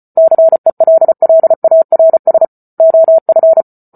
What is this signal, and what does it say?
Morse code, heard as a keyed tone, about 34 wpm, 650 Hz. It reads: CELLARS OF